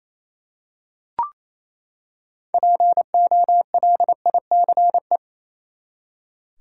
Morse code, heard as a keyed tone, about 28 wpm, 700 Hz. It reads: POLICE
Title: Six Letter Words - 28wpm